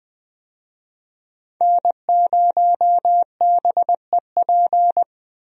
Morse code, heard as a keyed tone, about 20 wpm, 700 Hz. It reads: N0BEP
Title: USA Call Signs - 5 Characters - Repeated 3x - 20wpm